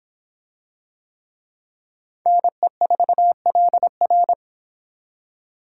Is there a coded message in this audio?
Yes: Morse code from 2.3 to 4.3 s, heard as a keyed tone, about 24 wpm, 700 Hz. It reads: NE4LR